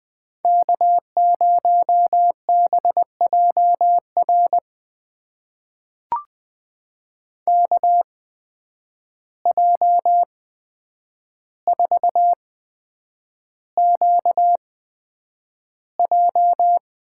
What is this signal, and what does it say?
Morse code, heard as a keyed tone, about 20 wpm, spaced as for 9 wpm, 700 Hz. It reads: K0BJR KJ4QJ
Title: USA Call Signs - 5 Characters - 8x Character Spacing - 20wpm